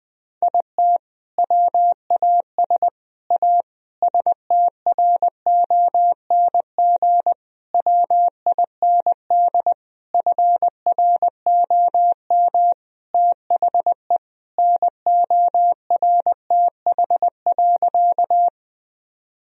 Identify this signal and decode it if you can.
Morse code, heard as a keyed tone, about 21 wpm, 700 Hz. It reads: IT WAS A STRONG WIND FROM THE NORTH.
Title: Sentences from Top 1000 Words - Review Part 2 - 20wpm